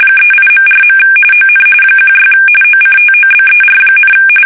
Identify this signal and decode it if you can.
RTTY, about 45.45 baud. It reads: 38.4. ON 16 OCTOBER,
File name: fsk850 / USS-Massachusetts